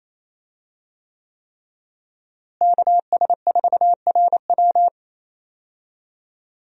Morse code, heard as a keyed tone, about 28 wpm, 700 Hz. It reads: KS4RW
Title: USA Call Signs - 5 Characters - Repeated 3x - 28wpm